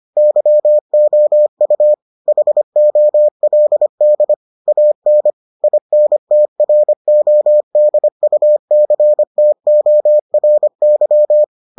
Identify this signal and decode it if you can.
Morse code, heard as a keyed tone, about 25 wpm, 600 Hz. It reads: YOU HOLD AN INTRODUCTORY